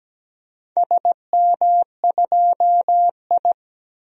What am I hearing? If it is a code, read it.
Morse code, heard as a keyed tone, about 17 wpm, 700 Hz. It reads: SM2I
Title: DXCC-100 Call Signs - 4 Characters - 17wpm